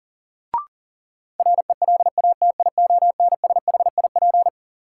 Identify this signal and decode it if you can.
Morse code, heard as a keyed tone, about 40 wpm, 700 Hz. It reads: RELATIONSHIP
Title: Sentences from Top 1100 Words - New Word Drill Alt - 40wpm